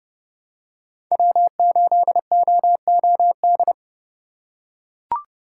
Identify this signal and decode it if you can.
Morse code, heard as a keyed tone, about 32 wpm, 700 Hz. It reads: W8OOD
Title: USA Call Signs - 5 Characters - 3x Character Spacing - 30wpm